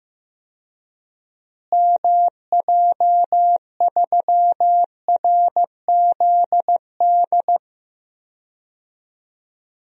Morse code, heard as a keyed tone, about 15 wpm, 700 Hz. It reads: MJ3RZD
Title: DXCC-100 Call Signs - 6 Characters - Repeated 3x - 15wpm